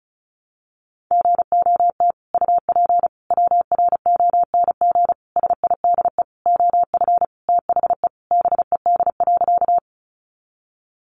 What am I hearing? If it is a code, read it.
Morse code, heard as a keyed tone, about 34 wpm, 700 Hz. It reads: GOT UP WRONG SIDE OF THE BED.
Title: Sentences from Top 900 Words 35wpm